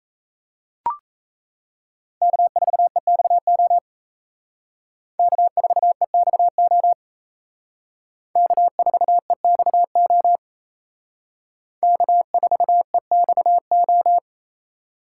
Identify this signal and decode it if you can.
Morse code, heard as a keyed tone, about 35 wpm, 700 Hz. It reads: K4EXO K4EXO K4EXO K4EXO
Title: Speed Racing - USA Call Signs 28wpm